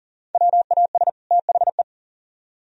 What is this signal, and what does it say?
Morse code, heard as a keyed tone, about 50 wpm, 700 Hz. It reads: WAS THE
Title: Speed-Racing - Sets of 2 Words 40wpm